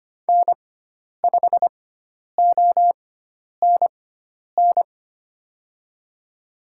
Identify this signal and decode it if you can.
Morse code, heard as a keyed tone, about 25 wpm, spaced as for 10 wpm, 700 Hz. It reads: N5ONN
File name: Lesson 16g - Letter c - Call Signs 25wpm